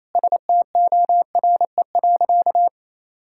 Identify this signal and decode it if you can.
Morse code, heard as a keyed tone, about 27 wpm, 700 Hz. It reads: STORE.
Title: Sentences from Top 1000 Words - Review Part 5 - 28wpm